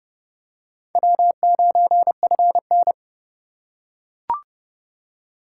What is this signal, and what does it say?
Morse code, heard as a keyed tone, about 30 wpm, 700 Hz. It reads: W9FN